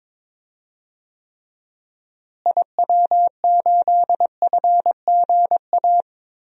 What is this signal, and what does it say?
Morse code, heard as a keyed tone, about 22 wpm, 700 Hz. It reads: IW8FGA
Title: DXCC-100 Call Signs - 6 Characters - Repeated 3x - 22wpm